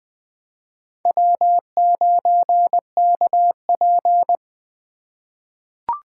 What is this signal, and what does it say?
Morse code, heard as a keyed tone, about 20 wpm, 700 Hz. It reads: W9KP
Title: USA Call Signs - 4x Character Spacing - 20wpm